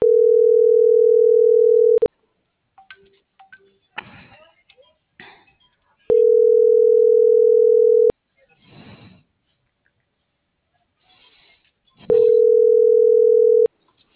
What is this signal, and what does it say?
Ringback tone (North America), 440 + 480 Hz.
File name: CENTRAL FRALSE / out fernanda.fralse